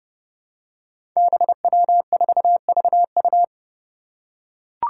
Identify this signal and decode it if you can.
Morse code, heard as a keyed tone, about 29 wpm, 700 Hz. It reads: BW4VU
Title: Five Challenging Letter-Number Combinations 30wpm